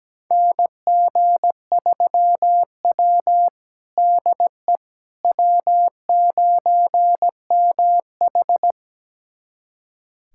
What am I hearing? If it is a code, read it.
Morse code, heard as a keyed tone, about 17 wpm, 700 Hz. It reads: NG3W DE W9MH
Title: USA Callsign DE Callsign - 4 Characters - 17wpm